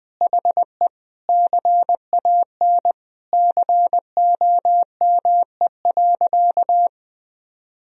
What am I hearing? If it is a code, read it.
Morse code, heard as a keyed tone, about 20 wpm, 700 Hz. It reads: HE CAN COME.
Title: Sentences from Top 1000 Words - Review Part 1 - 20wpm